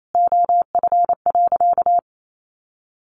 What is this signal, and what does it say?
Morse code, heard as a keyed tone, about 28 wpm, 700 Hz. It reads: OF.
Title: Sentences from Top 1500 Words 28wpm